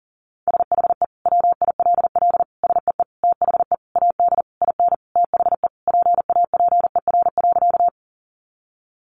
Morse code, heard as a keyed tone, about 37 wpm, 700 Hz. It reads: SHE WILL SEE THE AD IN THE PAPER.
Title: Sentences from Top 1300 Words 40wpm